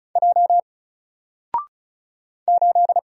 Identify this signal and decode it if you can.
Morse code, heard as a keyed tone, about 36 wpm, 700 Hz. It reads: J 8
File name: Single-Letter-Number-35wpm